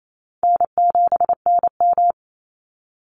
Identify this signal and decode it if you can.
Morse code, heard as a keyed tone, about 28 wpm, 700 Hz. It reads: N7NM
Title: Call Signs Repeated 2x - 28wpm